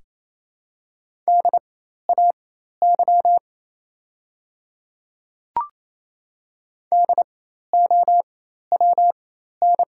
Morse code, heard as a keyed tone, about 28 wpm, spaced as for 13 wpm, 700 Hz. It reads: DAY DOWN
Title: 4x Character Spacing - Top 100 Words - 28wpm